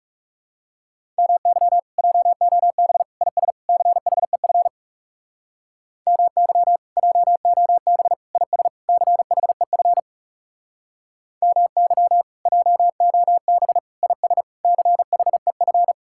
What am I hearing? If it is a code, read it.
Morse code, heard as a keyed tone, about 40 wpm, 700 Hz. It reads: MY JOB IS CHEF MY JOB IS CHEF MY JOB IS CHEF